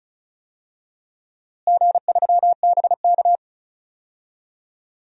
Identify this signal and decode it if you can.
Morse code, heard as a keyed tone, about 36 wpm, 700 Hz. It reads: G3BK
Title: DXCC-100 Call Signs - 4 Characters - Repeated 3x - 35wpm